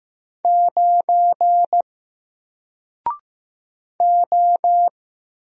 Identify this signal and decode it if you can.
Morse code, heard as a keyed tone, about 15 wpm, 700 Hz. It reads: 9 O
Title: Single Letter-Number Alt 15wpm